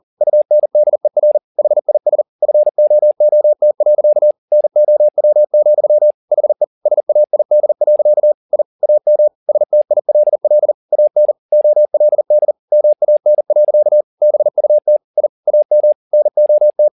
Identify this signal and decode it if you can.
Morse code, heard as a keyed tone, about 40 wpm, 600 Hz. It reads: UNDER HIS FOOT. NOW, HE SAID. I AM STILL AN OLD MAN. BUT I AM NOT
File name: the old man and the sea 2 40wpm